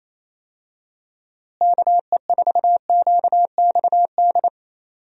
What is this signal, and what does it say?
Morse code, heard as a keyed tone, about 28 wpm, 700 Hz. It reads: KE4QXD
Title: USA Call Signs - 6 Characters - Repeated 3x - 28wpm